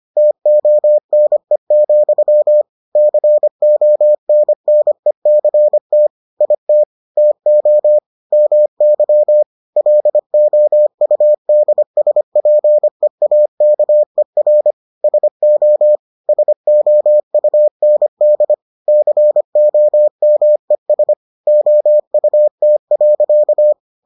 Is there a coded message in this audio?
Yes: Morse code from 0.2 to 23.7 s, heard as a keyed tone, about 25 wpm, 600 Hz. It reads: TONE, CONNECT IT TO MY LOUDSPEAKER SO SOUND COMES OUT.